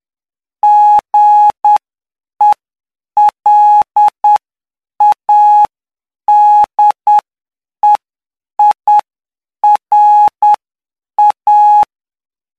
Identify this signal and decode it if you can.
Morse code, heard as a keyed tone, about 9 wpm, spaced as for 7 wpm, 840 Hz. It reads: GELADEIRA
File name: cw / py2lcd / 11-PY2LCD-Li��o 20